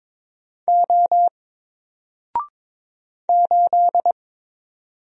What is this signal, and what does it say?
Morse code, heard as a keyed tone, about 22 wpm, 700 Hz. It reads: O 8
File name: Single-Letter-Number-22wpm